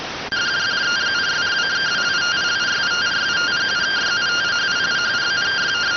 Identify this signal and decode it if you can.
Two-tone teleprinter signal, about 170 Hz shift.